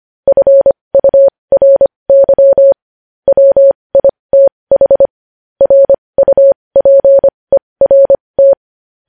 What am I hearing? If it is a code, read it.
Morse code, heard as a keyed tone, about 25 wpm, spaced as for 20 wpm, 550 Hz. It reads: FURY WITH RUPERT
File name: podcast / News headlines-1x-25x20